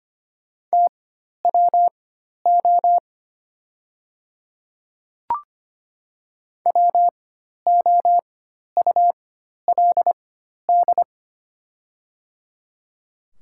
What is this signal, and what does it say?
Morse code, heard as a keyed tone, about 25 wpm, spaced as for 12 wpm, 700 Hz. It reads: TWO WOULD
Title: 4x Character Spacing - Top 200 Words - 25wpm